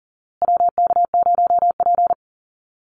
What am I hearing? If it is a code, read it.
Morse code, heard as a keyed tone, about 40 wpm, 700 Hz. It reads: WK0P
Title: Call Signs Repeated 2x - 40wpm